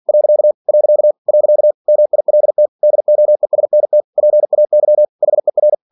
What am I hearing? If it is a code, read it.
Morse code, heard as a keyed tone, about 46 wpm, 600 Hz. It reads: . . MILT DOESNT PAY HER
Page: Never 47wpm